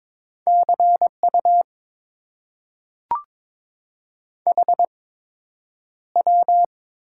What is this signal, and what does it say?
Morse code, heard as a keyed tone, about 22 wpm, spaced as for 9 wpm, 700 Hz. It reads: CU HW
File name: Top 90 QSO Elements - 8x Character Spacing - 22wpm